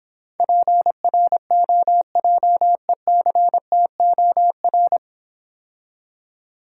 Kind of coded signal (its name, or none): Morse code